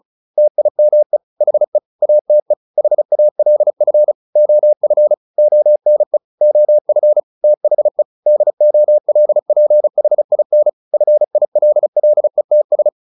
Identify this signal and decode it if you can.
Morse code, heard as a keyed tone, about 34 wpm, 600 Hz. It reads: TIME HE ATE HALF OF ONE OF THE DOLPHIN FILLETS
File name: the old man and the sea 2 35wpm